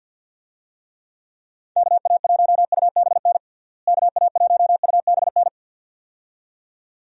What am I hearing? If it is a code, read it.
Morse code, heard as a keyed tone, about 51 wpm, 700 Hz. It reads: KA1UBN KA1UBN
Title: USA Call Signs - 6 Characters - Repeated 2x - 50wpm